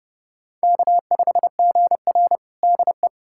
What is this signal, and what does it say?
Morse code, heard as a keyed tone, about 29 wpm, 700 Hz. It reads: K5GR DE
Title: USA Callsign DE Callsign 30wpm